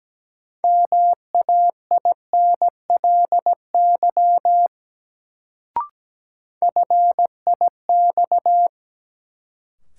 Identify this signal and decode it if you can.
Morse code, heard as a keyed tone, about 17 wpm, 700 Hz. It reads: MAINLY FIX
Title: Sentences from Top 1400 Words - New Word Drill - 17wpm